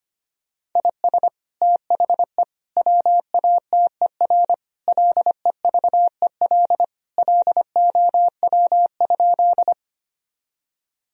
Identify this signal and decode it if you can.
Morse code, heard as a keyed tone, about 25 wpm, 700 Hz. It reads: IS THE WATER LEVEL LOW?
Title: Sentences from Top 1000 Words - Review Part 4 - 25wpm